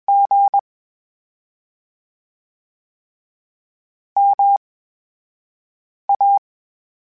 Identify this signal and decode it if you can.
Morse code, heard as a keyed tone, about 21 wpm, spaced as for 5 wpm, 800 Hz. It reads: G MA
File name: Special INGs-mp3s / Special INGs-mp3s / VerySlowToFaster Ing @20